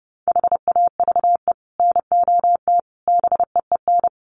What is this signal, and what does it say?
Morse code, heard as a keyed tone, about 28 wpm, 700 Hz. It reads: HAVE NOT BEEN
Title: Sets of 5 Words 30wpm